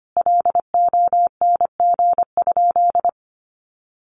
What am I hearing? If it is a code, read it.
Morse code, heard as a keyed tone, about 25 wpm, 700 Hz. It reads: LONG?
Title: Sentences from Top 200 Words 25wpm